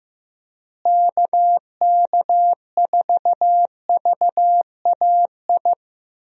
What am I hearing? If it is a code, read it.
Morse code, heard as a keyed tone, about 15 wpm, 700 Hz. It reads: KK4VAI
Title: USA Call Signs - 6 Characters - Repeated 2x - 15wpm